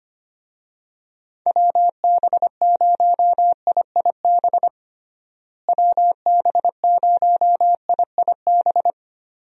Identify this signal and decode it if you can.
Morse code, heard as a keyed tone, about 25 wpm, 700 Hz. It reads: WB0IIB WB0IIB
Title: USA Call Signs - 6 Characters - Repeated 3x - 25wpm